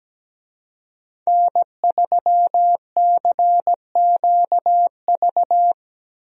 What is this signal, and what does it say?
Morse code, heard as a keyed tone, about 17 wpm, 700 Hz. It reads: N3CQV
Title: USA Call Signs - 5 Characters - Repeated 2x - 17wpm